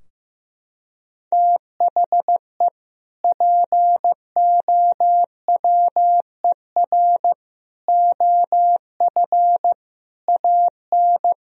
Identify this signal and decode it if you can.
Morse code, heard as a keyed tone, about 15 wpm, 700 Hz. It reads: THE POWER OF AN
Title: Sentences from Top 1000 Words - Review Part 4 - 15wpm